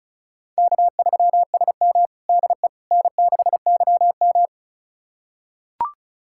Morse code, heard as a keyed tone, about 36 wpm, 700 Hz. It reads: K3SM DE N6YM
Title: USA Callsign DE Callsign - 4 Characters - 35wpm